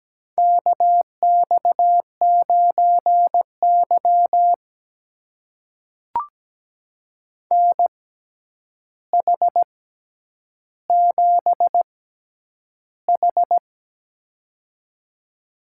Morse code, heard as a keyed tone, about 17 wpm, spaced as for 9 wpm, 700 Hz. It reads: KX9Y NH7H
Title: USA Call Signs - 4 Characters - 6x Character Spacing - 17wpm